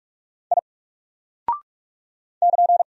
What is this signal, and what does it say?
Morse code, heard as a keyed tone, about 44 wpm, 700 Hz. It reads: I <KN>
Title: Single Character (Letter:Number:Punctuation:Prosign) 45wpm